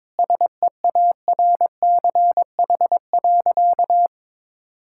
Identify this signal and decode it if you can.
Morse code, heard as a keyed tone, about 22 wpm, 700 Hz. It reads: SEARCH.
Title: Sentences from Top 1000 Words - Review Part 5 - 22wpm